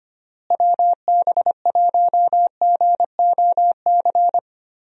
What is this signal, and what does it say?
Morse code, heard as a keyed tone, about 25 wpm, 700 Hz. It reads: WB1GOC
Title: USA Call Signs - 6 Characters - Repeated 3x - 25wpm